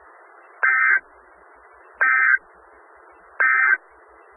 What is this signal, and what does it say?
Emergency Alert System (SAME) tones, heard as harsh buzzing bursts. It is heard from 0.6 to 3.8 s.
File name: Skywarn/2013/2013-03-18-Newton County WA4ASI Repeater